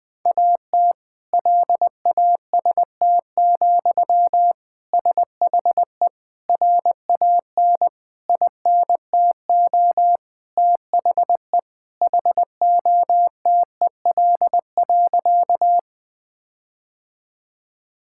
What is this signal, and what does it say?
Morse code, heard as a keyed tone, about 20 wpm, 700 Hz. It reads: AT LAST, SHE RAN INTO THE HOTEL.
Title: Sentences from Top 1600 Words 20wpm